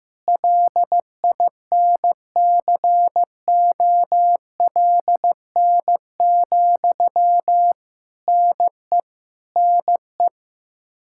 Morse code, heard as a keyed tone, about 16 wpm, 700 Hz. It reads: LINCOLN, NE NE
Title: Practice QSO - Exchange 1 - Beginning - 15wpm